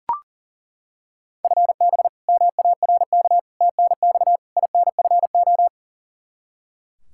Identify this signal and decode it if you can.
Morse code, heard as a keyed tone, about 41 wpm, 700 Hz. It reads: FB MARK TNX INFO
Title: K1USN SST QSO Practice 40wpm